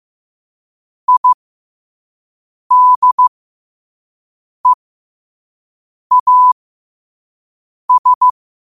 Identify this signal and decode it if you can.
Morse code, heard as a keyed tone, about 15 wpm, spaced as for 5 wpm, 1000 Hz. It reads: IDEAS
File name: qotdfiles / 5wpm